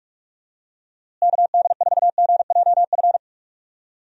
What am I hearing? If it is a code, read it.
Morse code, heard as a keyed tone, about 46 wpm, 700 Hz. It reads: KD4GJF
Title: USA Call Signs - 6 Characters - Repeated 3x - 45wpm